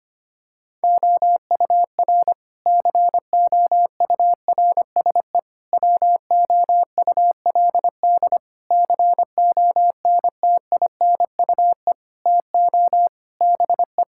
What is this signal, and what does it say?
Morse code, heard as a keyed tone, about 25 wpm, 700 Hz. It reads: OUR COURSE WOULD CONTINUE TO BE
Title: Sentences from Top 1000 Words - Review Part 6 - 25wpm